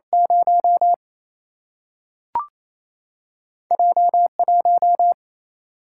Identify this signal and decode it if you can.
Morse code, heard as a keyed tone, about 28 wpm, 700 Hz. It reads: 0 J1
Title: Single-Letter Single-Number 28wpm